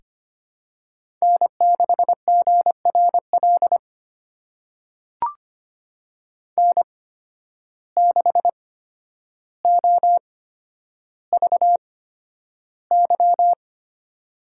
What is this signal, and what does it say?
Morse code, heard as a keyed tone, about 25 wpm, spaced as for 11 wpm, 700 Hz. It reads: N6GRL N6OVY
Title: USA Call Signs - 8x Character Spacing - 25wpm